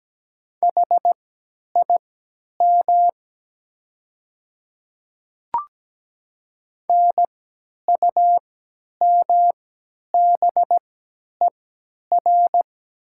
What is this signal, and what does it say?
Morse code, heard as a keyed tone, about 17 wpm, spaced as for 10 wpm, 700 Hz. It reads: HIM NUMBER
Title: 3x Character Spacing - Top 100 Words - 17wpm